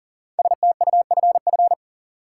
Morse code, heard as a keyed tone, about 40 wpm, 700 Hz. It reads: STUFF
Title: Sentences from Top 1400 Words - New Word Drill Alt - 40wpm